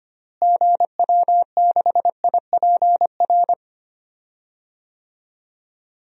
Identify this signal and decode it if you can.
Morse code, heard as a keyed tone, about 25 wpm, 700 Hz. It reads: GW6IPR
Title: DXCC-100 Call Signs - 6 Characters - Repeated 3x - 25wpm